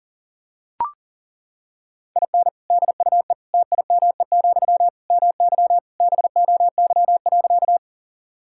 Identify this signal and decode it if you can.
Morse code, heard as a keyed tone, about 37 wpm, 700 Hz. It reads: IN DUE TIME, MY BOY.
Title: Sentences from Top 1000 Words - Review Part 6 - 40wpm